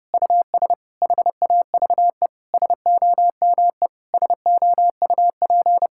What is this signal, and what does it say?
Morse code, heard as a keyed tone, about 32 wpm, 700 Hz. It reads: US HAVE SOME SOUP
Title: Sentences from Top 1000 Words - Review Part 6 - 30wpm